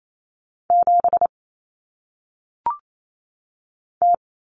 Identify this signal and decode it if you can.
Morse code, heard as a keyed tone, about 28 wpm, 700 Hz. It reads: 7 T
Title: Common Cut Numbers - 1-Digit 28wpm